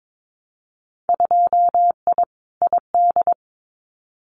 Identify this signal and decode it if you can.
Morse code, heard as a keyed tone, about 22 wpm, 700 Hz. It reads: 2I ID
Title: Winter Field Day Exchanges Encore 22wpm